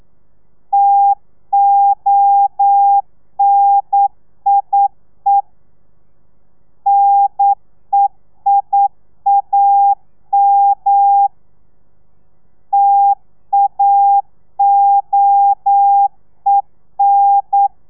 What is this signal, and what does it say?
Morse code, heard as a keyed tone, about 9 wpm, 800 Hz. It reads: TONIE NEIAM TAOEN
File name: telegrafia / i2tfj / Lez2 40